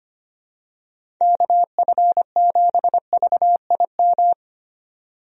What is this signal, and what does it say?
Morse code, heard as a keyed tone, about 25 wpm, 700 Hz. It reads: KF7VIM
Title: USA Call Signs - 6 Characters - 25wpm